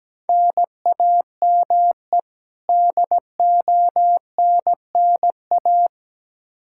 Morse code, heard as a keyed tone, about 17 wpm, 700 Hz. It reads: NAME DONNA
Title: Practice QSO - Exchange 1 - Beginning - 17wpm